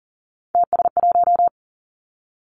Morse code, heard as a keyed tone, about 38 wpm, 690 Hz. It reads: TS1
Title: Three Challenging Letter-Number Combinations 40wpm